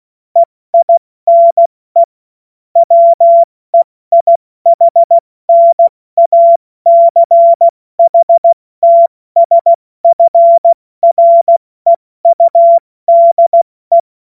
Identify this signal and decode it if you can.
Morse code, heard as a keyed tone, about 16 wpm, spaced as for 14 wpm, 680 Hz. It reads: EINE WEIHNACHTSFREUDE
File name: Morsekurs / Witze / Witz 14. März 2023 70 F80